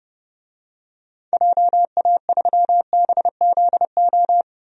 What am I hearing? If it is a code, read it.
Morse code, heard as a keyed tone, about 31 wpm, 700 Hz. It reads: JA3BZO
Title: DXCC-100 Call Signs - 6 Characters - 30wpm